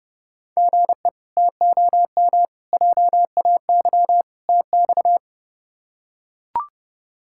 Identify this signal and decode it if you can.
Morse code, heard as a keyed tone, about 31 wpm, 700 Hz. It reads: GE TOM JAY TX
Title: K1USN SST QSO Practice - 30wpm